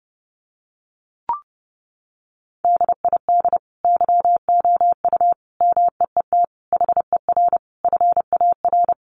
Morse code, heard as a keyed tone, about 30 wpm, 700 Hz. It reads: DID YOU MEET HER FAR
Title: Sentences from Top 1100 Words 30wpm